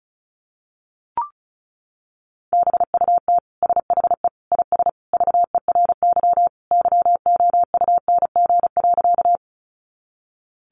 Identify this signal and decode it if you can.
Morse code, heard as a keyed tone, about 34 wpm, 700 Hz. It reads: BUT SHE IS VERY YOUNG.
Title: Sentences from Top 400 Words 35wpm